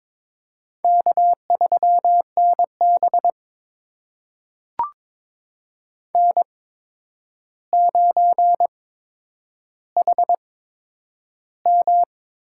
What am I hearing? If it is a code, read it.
Morse code, heard as a keyed tone, about 22 wpm, spaced as for 9 wpm, 700 Hz. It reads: K3NB N9HM
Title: USA Call Signs - 4 Characters - 8x Character Spacing - 22wpm